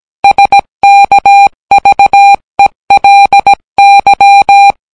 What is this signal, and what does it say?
Morse code, heard as a keyed tone, about 17 wpm, 800 Hz. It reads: SKVELY